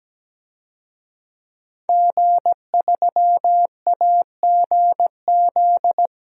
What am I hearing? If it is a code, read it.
Morse code, heard as a keyed tone, about 17 wpm, 700 Hz. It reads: G3AGZ
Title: DXCC-100 Call Signs - 5 Characters - 17wpm